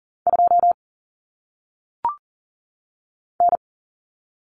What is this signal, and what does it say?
Morse code, heard as a keyed tone, about 40 wpm, 700 Hz. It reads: 2 N
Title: Common Cut Numbers - 1-Digit 40wpm